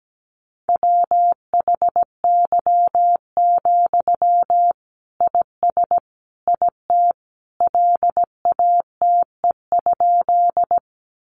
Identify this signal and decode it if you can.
Morse code, heard as a keyed tone, about 17 wpm, 700 Hz. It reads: WHY, IS IT LATE?
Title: Sentences from Top 300 Words 17wpm